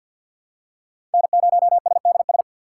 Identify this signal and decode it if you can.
Morse code, heard as a keyed tone, about 49 wpm, 700 Hz. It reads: N0SDS
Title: USA Call Signs - 5 Characters - 3x Character Spacing - 50wpm